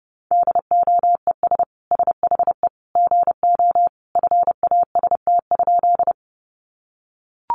Morse code, heard as a keyed tone, about 30 wpm, 700 Hz. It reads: DOES SHE GO FAST?
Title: Sentences from Top 400 Words 30wpm